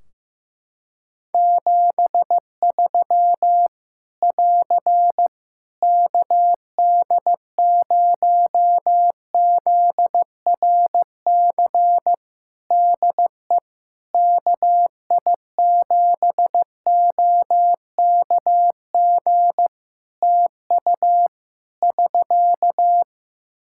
Morse code, heard as a keyed tone, about 15 wpm, 700 Hz. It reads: 73 <AR> KD0ZRC DE KI7OKG TU <SK>
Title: K1USN SST QSO Practice 15wpm